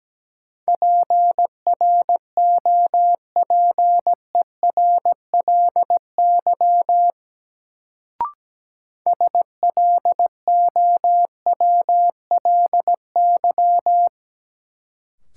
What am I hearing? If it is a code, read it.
Morse code, heard as a keyed tone, about 17 wpm, 700 Hz. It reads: PROPERLY SLOWLY
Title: Sentences from Top 1100 Words - New Word Drill - 17wpm